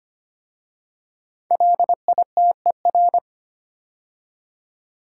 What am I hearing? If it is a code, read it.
Morse code, heard as a keyed tone, about 25 wpm, 700 Hz. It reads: LITER
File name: 5 Letter Words - First 3 Letters Given - 25wpm